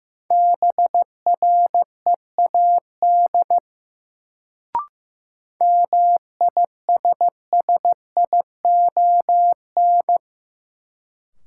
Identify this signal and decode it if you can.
Morse code, heard as a keyed tone, about 15 wpm, 700 Hz. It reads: BREAD MISSION
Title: Sentences from Top 1600 Words - New Word Drill - 15wpm